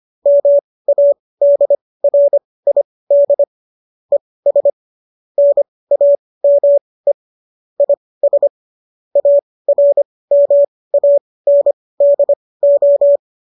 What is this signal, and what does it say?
Morse code, heard as a keyed tone, about 25 wpm, spaced as for 18 wpm, 570 Hz. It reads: MADRID ES NAME IS ARMANDO
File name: qso215 18